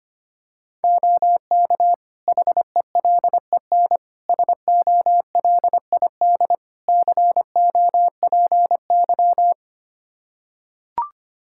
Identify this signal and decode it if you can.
Morse code, heard as a keyed tone, about 25 wpm, 700 Hz. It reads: OK HELEN SOLID COPY
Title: Practice QSO - Exchange 3 - More Info - 25wpm